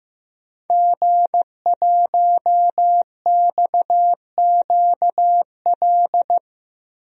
Morse code, heard as a keyed tone, about 15 wpm, 700 Hz. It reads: G1XQL
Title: DXCC-100 Call Signs - 5 Characters - Repeated 3x - 15wpm